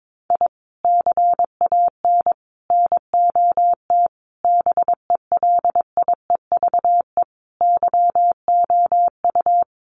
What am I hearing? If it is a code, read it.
Morse code, heard as a keyed tone, about 22 wpm, 700 Hz. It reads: I CAN NOT BELIEVE YOU
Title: Sets of 5 Words Encore 22wpm